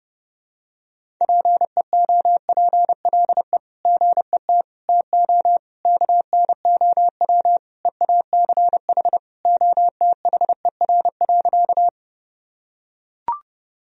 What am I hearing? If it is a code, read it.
Morse code, heard as a keyed tone, about 30 wpm, 700 Hz. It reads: PEOPLE GET TO KNOW EACH OTHER.
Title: Sentences from Top 1000 Words - Review Part 1 - 30wpm